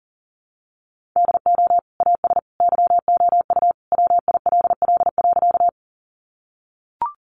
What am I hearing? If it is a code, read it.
Morse code, heard as a keyed tone, about 40 wpm, 700 Hz. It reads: DO AS YOU WILL.
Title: Sentences from Top 100 Words 40wpm